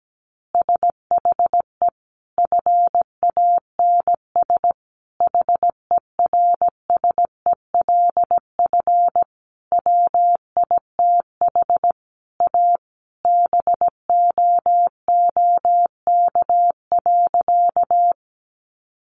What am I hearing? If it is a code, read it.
Morse code, heard as a keyed tone, about 17 wpm, 700 Hz. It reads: SHE FANS HERSELF WITH A BOOK.
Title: Easy Phrases 17wpm